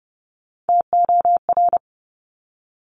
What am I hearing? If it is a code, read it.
Morse code, heard as a keyed tone, about 30 wpm, 700 Hz. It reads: TOR
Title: Top Three Letter Combinations in Words 30wpm